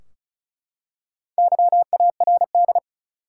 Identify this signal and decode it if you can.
Morse code, heard as a keyed tone, about 34 wpm, 700 Hz. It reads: YARD
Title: Four Letter Words - 35wpm